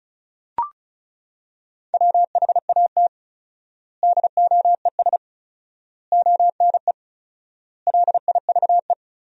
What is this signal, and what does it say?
Morse code, heard as a keyed tone, about 36 wpm, 700 Hz. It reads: WHAT DOES ONE LIVE
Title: 4x Word-Spacing - Sentences from Top 200 Words - 35wpm